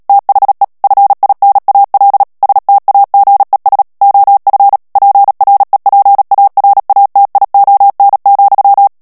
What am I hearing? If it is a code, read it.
Morse code, heard as a keyed tone, about 37 wpm, 800 Hz. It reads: THE FINAL STAGES OF PREPARATION,
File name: morse mp3 / ANGLAIS texte clair 3 vitesse 35 mots minute